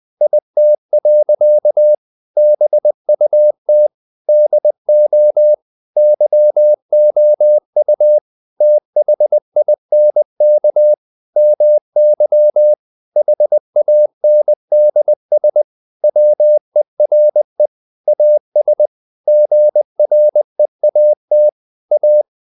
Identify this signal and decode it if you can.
Morse code, heard as a keyed tone, about 20 wpm, 600 Hz. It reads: IT. BUT DO YOU THINK MY HANDS WERE AS GREAT A
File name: the old man and the sea part 2 20wpm